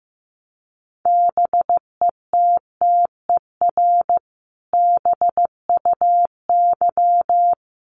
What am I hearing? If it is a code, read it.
Morse code, heard as a keyed tone, about 15 wpm, 700 Hz. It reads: BETTER BUY
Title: Sentences from Top 700 Words 15wpm